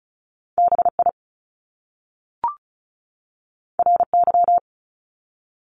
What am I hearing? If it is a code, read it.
Morse code, heard as a keyed tone, about 36 wpm, 700 Hz. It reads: BI RY
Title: Top Two Letter Combinations in Words 35wpm